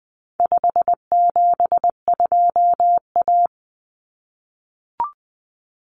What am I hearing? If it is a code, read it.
Morse code, heard as a keyed tone, about 20 wpm, 700 Hz. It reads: 572A
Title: Common Cut Numbers - 4-Digit 20wpm